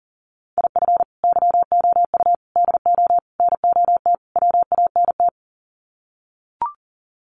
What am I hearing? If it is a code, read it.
Morse code, heard as a keyed tone, about 40 wpm, 700 Hz. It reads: IF YOU DO NOT WANT
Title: Sets of 5 Words 40wpm